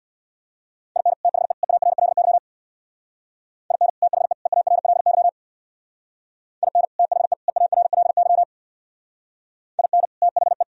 Morse code, heard as a keyed tone, about 62 wpm, 700 Hz. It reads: IN THE EARLY IN THE EARLY IN THE EARLY IN THE
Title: Speed-Racing - Sets of 3 Words 50wpm